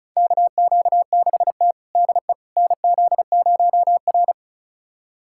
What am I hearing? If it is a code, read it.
Morse code, heard as a keyed tone, about 34 wpm, 700 Hz. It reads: KQ6T DE NZ0R
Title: USA Callsign DE Callsign - 4 Characters - 35wpm